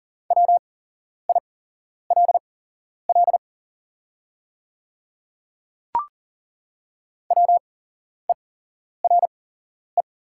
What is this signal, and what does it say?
Morse code, heard as a keyed tone, about 40 wpm, spaced as for 11 wpm, 690 Hz. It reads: WILL WERE
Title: 8x Character Spacing - Top 100 Words - 40wpm